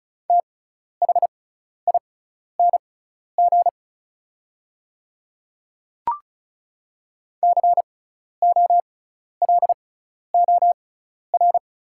Morse code, heard as a keyed tone, about 36 wpm, spaced as for 12 wpm, 690 Hz. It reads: THING COLOR
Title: 6x Character Spacing - Top 300 Words - 35wpm